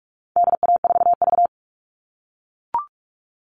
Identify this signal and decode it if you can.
Morse code, heard as a keyed tone, about 44 wpm, 700 Hz. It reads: DA4V
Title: All Cut Numbers - 4-Digit 45wpm